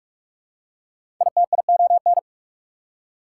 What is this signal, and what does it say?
Morse code, heard as a keyed tone, about 46 wpm, 690 Hz. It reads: ITION
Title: Prefixes and Suffixes 45wpm